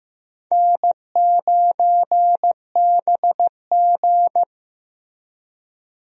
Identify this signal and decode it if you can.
Morse code, heard as a keyed tone, about 15 wpm, 700 Hz. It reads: N9BG
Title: USA Call Signs - 4 Characters - Repeated 3x - 15wpm